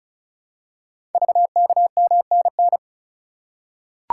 Morse code, heard as a keyed tone, about 36 wpm, 700 Hz. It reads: VKMNN
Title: Code Groups - 5 Letters - 35wpm